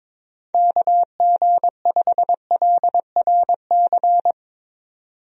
Morse code, heard as a keyed tone, about 21 wpm, 700 Hz. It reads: KG5LRC
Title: USA Call Signs - 6 Characters - 22wpm